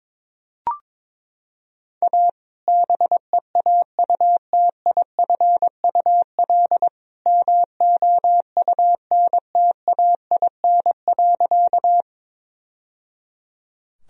Morse code, heard as a keyed tone, about 22 wpm, 700 Hz. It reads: A BEAUTIFUL MOUNTAIN.
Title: Sentences from Top 1000 Words - Review Part 1 - 22wpm